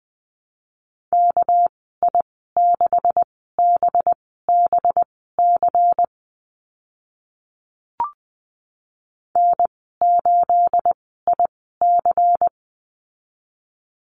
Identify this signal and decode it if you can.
Morse code, heard as a keyed tone, about 20 wpm, spaced as for 15 wpm, 700 Hz. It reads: KI6BBC N8IC
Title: 213 Lesson 34g - Number 8 - Call Signs 20wpm